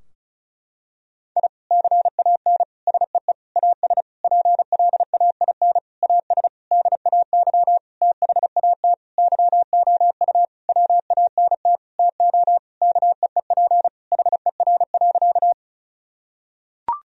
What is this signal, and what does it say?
Morse code, heard as a keyed tone, about 34 wpm, 700 Hz. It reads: I CAN SEE AS PLAIN AS DAY THAT YOU WANT TO KEEP HER.
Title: Sentences from Top 1000 Words - Review Part 2 - 35wpm